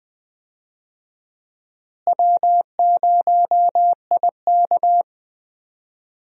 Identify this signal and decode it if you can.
Morse code, heard as a keyed tone, about 20 wpm, 700 Hz. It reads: W0IK
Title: USA Call Signs - 4 Characters - Repeated 3x - 20wpm